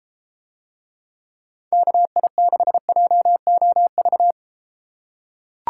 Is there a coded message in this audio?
Yes: Morse code from 1.7 to 4.3 s, heard as a keyed tone, about 31 wpm, 700 Hz. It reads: KI6JOV